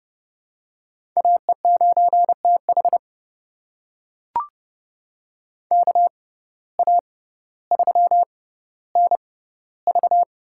Morse code, heard as a keyed tone, about 31 wpm, spaced as for 16 wpm, 700 Hz. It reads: AE9TH KA3NV
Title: USA Call Signs - 6x Character Spacing - 30wpm